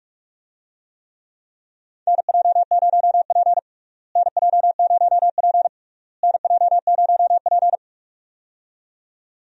Morse code, heard as a keyed tone, about 45 wpm, 700 Hz. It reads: NJ0P NJ0P NJ0P
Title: USA Call Signs - 4 Characters - Repeated 3x - 45wpm